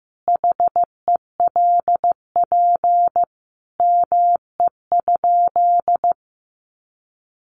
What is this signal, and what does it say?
Morse code, heard as a keyed tone, about 15 wpm, 700 Hz. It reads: HELP ME?
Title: Sentences from Top 200 Words 15wpm